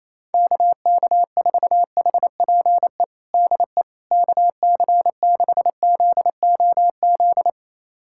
Morse code, heard as a keyed tone, about 28 wpm, 700 Hz. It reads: KK4HPE DE KC6ZOZ
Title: USA Callsign DE Callsign - 6 Characters - 28wpm